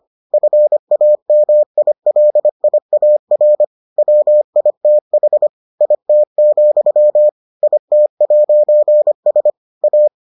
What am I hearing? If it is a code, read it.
Morse code, heard as a keyed tone, about 25 wpm, 600 Hz. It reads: FAMILIAR WITH IT, IT'S A